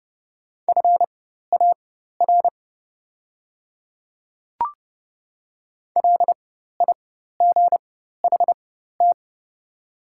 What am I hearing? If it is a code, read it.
Morse code, heard as a keyed tone, about 30 wpm, spaced as for 14 wpm, 700 Hz. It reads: FAR LIGHT
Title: 4x Character Spacing - Top 300 Words - 30wpm